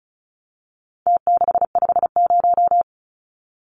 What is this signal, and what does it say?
Morse code, heard as a keyed tone, about 34 wpm, 700 Hz. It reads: T650
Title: Numbers - Contest Style - 4-Digit 35wpm